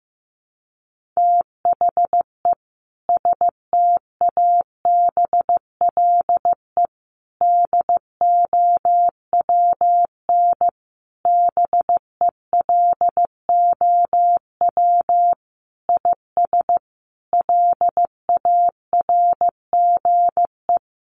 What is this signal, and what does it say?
Morse code, heard as a keyed tone, about 14 wpm, 700 Hz. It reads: THE STABLE DOWN BELOW IS LARGE
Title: Sentences from Top 1500 Words 15wpm